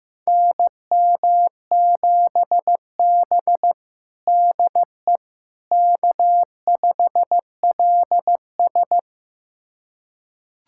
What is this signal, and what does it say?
Morse code, heard as a keyed tone, about 15 wpm, 700 Hz. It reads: NM7B DE K5LS
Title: USA Callsign DE Callsign - 4 Characters - 15wpm